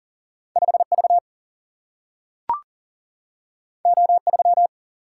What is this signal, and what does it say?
Morse code, heard as a keyed tone, about 40 wpm, 700 Hz. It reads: 5 V O 3
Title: Single-Letter Single-Number 40wpm